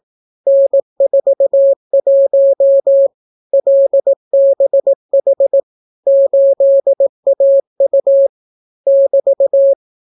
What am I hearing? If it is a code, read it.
Morse code, heard as a keyed tone, about 18 wpm, 550 Hz. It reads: N41 LBH 8AU <BT>